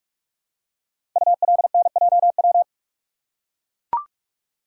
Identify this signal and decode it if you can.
Morse code, heard as a keyed tone, about 44 wpm, 690 Hz. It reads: ULNJW
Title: Code Groups - 5 Letters - 45wpm